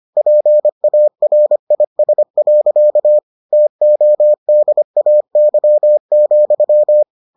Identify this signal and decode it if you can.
Morse code, heard as a keyed tone, about 25 wpm, 600 Hz. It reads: PARIS. TODAY,